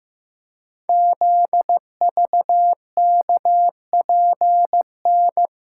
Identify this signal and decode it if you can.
Morse code, heard as a keyed tone, about 15 wpm, 700 Hz. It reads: ZVKPN
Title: Code Groups - 5 Letters - 15wpm